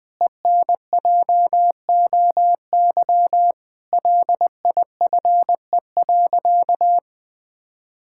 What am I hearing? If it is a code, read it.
Morse code, heard as a keyed tone, about 20 wpm, 700 Hz. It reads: ENJOY LIFE.
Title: Sentences with 8-Word Phrases - 20wpm